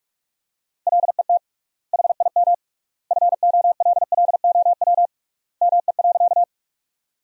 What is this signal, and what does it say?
Morse code, heard as a keyed tone, about 44 wpm, 700 Hz. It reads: LET HIM FOLLOW ME.
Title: 3x Word-Spacing - Sentences from Top 300 Words - 45wpm